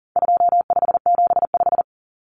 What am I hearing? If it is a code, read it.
Morse code, heard as a keyed tone, about 40 wpm, 700 Hz. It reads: 2575
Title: Numbers - 4-Digit 40wpm